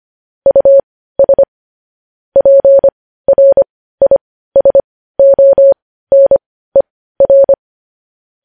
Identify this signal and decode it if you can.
Morse code, heard as a keyed tone, about 25 wpm, spaced as for 15 wpm, 550 Hz. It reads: US PRISONER